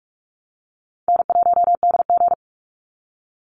Morse code, heard as a keyed tone, about 46 wpm, 700 Hz. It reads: N1DG
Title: Easy Call Signs - 45wpm